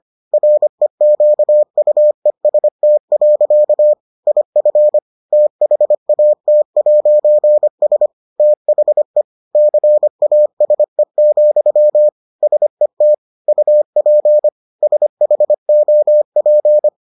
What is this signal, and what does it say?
Morse code, heard as a keyed tone, about 25 wpm, 600 Hz. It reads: REQUEST. IF THAT'S THE CASE, SET UP SHOP